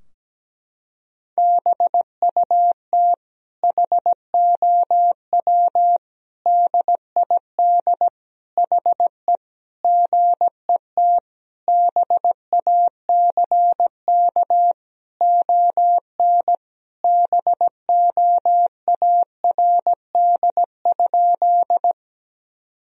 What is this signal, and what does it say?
Morse code, heard as a keyed tone, about 17 wpm, 700 Hz. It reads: BUT HOW DID HE GET BACK ON BOARD?
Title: Sentences from Top 1000 Words - Review Part 4 - 17wpm